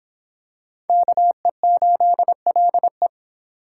Morse code, heard as a keyed tone, about 25 wpm, 700 Hz. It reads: KE8LE